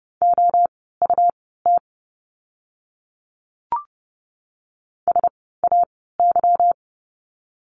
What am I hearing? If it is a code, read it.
Morse code, heard as a keyed tone, about 30 wpm, spaced as for 17 wpm, 700 Hz. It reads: OUT SAY
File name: course / 30:10wpm / 230 Lesson 37f - Letter x - Words 30wpm